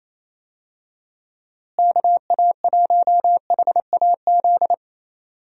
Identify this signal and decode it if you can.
Morse code, heard as a keyed tone, about 28 wpm, 700 Hz. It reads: KA1HAZ
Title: USA Call Signs - 6 Characters - Repeated 2x - 28wpm